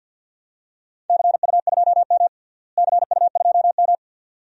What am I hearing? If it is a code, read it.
Morse code, heard as a keyed tone, about 52 wpm, 700 Hz. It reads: CU2M CU2M
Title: DXCC-100 Call Signs - 4 Characters - Repeated 2x - 50wpm